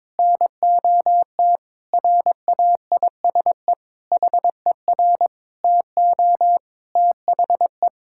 Morse code, heard as a keyed tone, about 22 wpm, 700 Hz. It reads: NOT RAISE HER TO THE
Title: Sentences from Top 1600 Words 22wpm